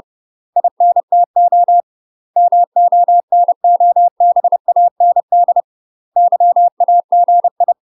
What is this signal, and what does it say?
Morse code, heard as a keyed tone, about 30 wpm, 700 Hz. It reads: INTO MONOBAND YAGI
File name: UK MASTER QSOs TRAINING / 30WPM / UK MASTER QSOs 0005